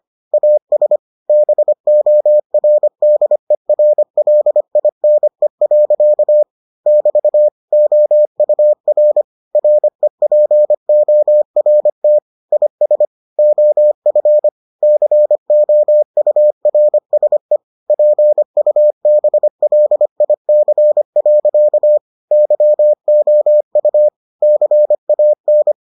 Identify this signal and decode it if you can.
Morse code, heard as a keyed tone, about 25 wpm, 600 Hz. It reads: AS BORDERLINE. <BT> OUR REPORT IS OF COURSE PUBLIC. YOU CAN